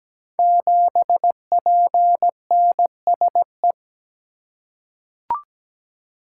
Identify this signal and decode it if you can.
Morse code, heard as a keyed tone, about 17 wpm, 700 Hz. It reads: 7PNSE
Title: Code Groups - 5 Letter:Numbers - 17wpm